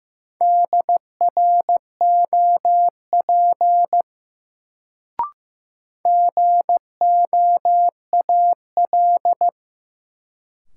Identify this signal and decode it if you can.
Morse code, heard as a keyed tone, about 15 wpm, 700 Hz. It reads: DROP GOAL
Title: Sentences from Top 1400 Words - New Word Drill Alt - 15wpm